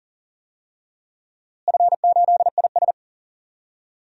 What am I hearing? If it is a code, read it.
Morse code, heard as a keyed tone, about 40 wpm, 700 Hz. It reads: F8IS
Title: DXCC-100 Call Signs - 4 Characters - Repeated 3x - 40wpm